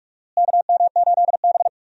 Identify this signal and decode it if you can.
Morse code, heard as a keyed tone, about 46 wpm, 700 Hz. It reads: KM8B
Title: USA Call Signs - 4 Characters - Repeated 2x - 45wpm